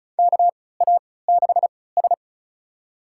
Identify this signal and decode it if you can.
Morse code, heard as a keyed tone, about 36 wpm, spaced as for 20 wpm, 700 Hz. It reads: KA6S
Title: USA Call Signs - 4 Characters - 3x Character Spacing - 35wpm